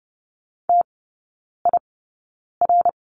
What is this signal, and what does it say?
Morse code, heard as a keyed tone, about 30 wpm, spaced as for 9 wpm, 690 Hz. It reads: TIR
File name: Lesson 16e - Letter c - Three Characters 30wpm